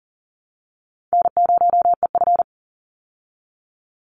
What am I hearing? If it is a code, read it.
Morse code, heard as a keyed tone, about 40 wpm, 700 Hz. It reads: N0EF
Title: Easy Call Signs Repeated 3x - 40wpm